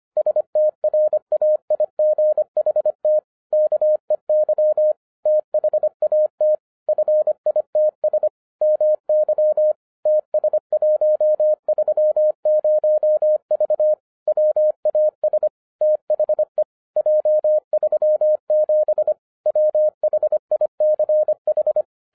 Morse code, heard as a keyed tone, about 24 wpm, 610 Hz. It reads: STRAIGHT KEY THAT FITS MY TS130V WAS THE J37 WHICH
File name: FIRST FISTS CONTACT-25wpm